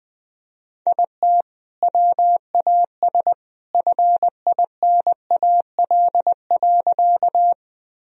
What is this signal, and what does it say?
Morse code, heard as a keyed tone, about 20 wpm, 700 Hz. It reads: IT WAS FINAL.
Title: Sentences from Top 1000 Words - Review Part 2 - 20wpm